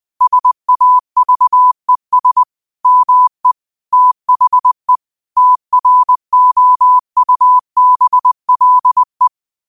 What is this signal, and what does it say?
Morse code, heard as a keyed tone, about 20 wpm, 1000 Hz. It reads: SAVES ME THE TROUBLE
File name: qotdfiles / 20wpm